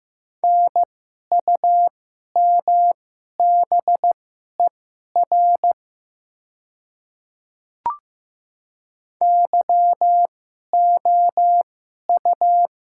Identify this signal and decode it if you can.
Morse code, heard as a keyed tone, about 15 wpm, spaced as for 11 wpm, 700 Hz. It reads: NUMBER YOU
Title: 2x Character Spacing - Top 100 Words - 15wpm